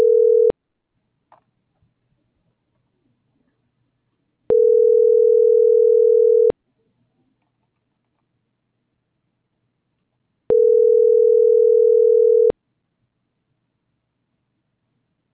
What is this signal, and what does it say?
Ringback tone (North America), 440 + 480 Hz.